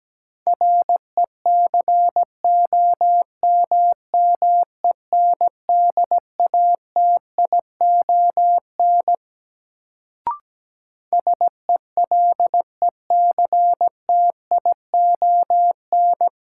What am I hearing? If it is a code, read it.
Morse code, heard as a keyed tone, about 17 wpm, 700 Hz. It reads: RECOMMENDATION SELECTION
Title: Sentences from Top 1400 Words - New Word Drill - 17wpm